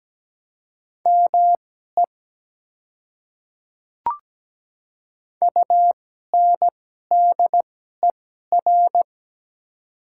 Morse code, heard as a keyed tone, about 17 wpm, spaced as for 12 wpm, 700 Hz. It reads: ME UNDER
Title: 2x Character Spacing - Top 200 Words - 17wpm